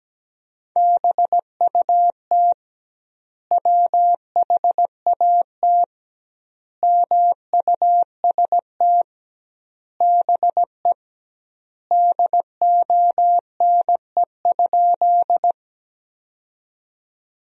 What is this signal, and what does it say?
Morse code, heard as a keyed tone, about 17 wpm, 700 Hz. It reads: BUT WHAT MUST BE DONE?
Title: 2x Word-Spacing - Sentences from Top 500 Words - 17wpm